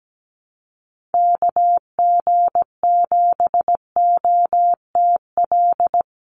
Morse code, heard as a keyed tone, about 17 wpm, 700 Hz. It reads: KG7OTL
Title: USA Call Signs 17wpm